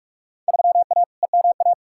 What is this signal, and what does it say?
Morse code, heard as a keyed tone, about 42 wpm, 700 Hz. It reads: 3A EMA
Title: Field Day Exchanges - 45wpm